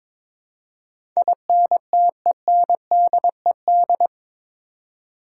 Morse code, heard as a keyed tone, about 22 wpm, 700 Hz. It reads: INTENDED
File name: practice / 22wpm / Eight Letter Words - 22wpm